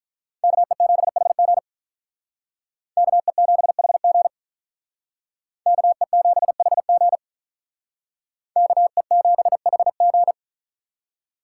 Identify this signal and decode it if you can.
Morse code, heard as a keyed tone, about 46 wpm, 700 Hz. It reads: KE7HG KE7HG KE7HG KE7HG